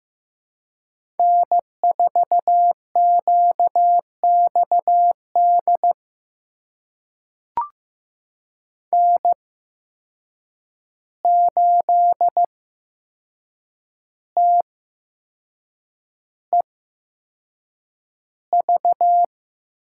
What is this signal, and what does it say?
Morse code, heard as a keyed tone, about 15 wpm, spaced as for 6 wpm, 700 Hz. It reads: N4QXD N8TEV
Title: USA Call Signs - 5 Characters - 8x Character Spacing - 15wpm